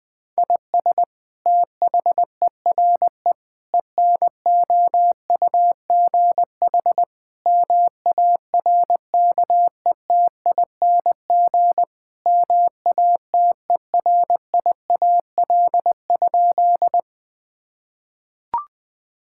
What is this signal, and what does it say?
Morse code, heard as a keyed tone, about 21 wpm, 700 Hz. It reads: IS THERE ENOUGH MARKETING MATERIAL?
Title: Sentences from Top 1000 Words - Review Part 5 - 20wpm